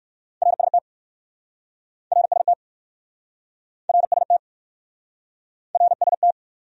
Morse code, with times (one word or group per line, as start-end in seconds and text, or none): 0.4-0.8 RST
2.1-2.5 RST
3.9-4.4 RST
5.7-6.3 RST